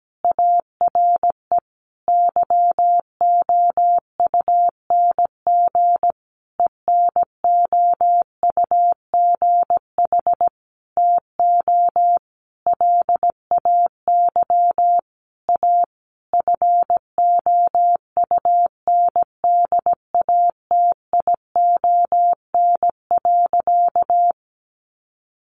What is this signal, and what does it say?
Morse code, heard as a keyed tone, about 17 wpm, 700 Hz. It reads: ARE YOUNG ENOUGH TO LAY A FOUNDATION.
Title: Sentences from Top 1200 Words 17wpm